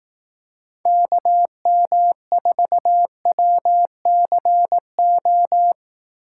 Morse code, heard as a keyed tone, about 18 wpm, 700 Hz. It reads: KM4WCO
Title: Speed Racing - USA 2x3 Call Signs 15wpm